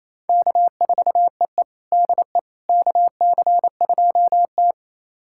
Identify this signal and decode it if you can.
Morse code, heard as a keyed tone, about 29 wpm, 690 Hz. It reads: K4EE DE KC2T
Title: USA Callsign DE Callsign - 4 Characters - 28wpm